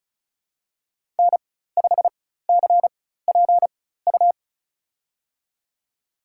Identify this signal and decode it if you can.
Morse code, heard as a keyed tone, about 34 wpm, spaced as for 16 wpm, 700 Hz. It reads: N5CPU
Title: USA Call Signs - 5 Characters - 4x Character Spacing - 35wpm